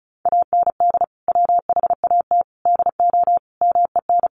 Morse code, heard as a keyed tone, about 34 wpm, 700 Hz. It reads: AND WHAT DO MEN
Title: Sentences from Top 1100 Words 35wpm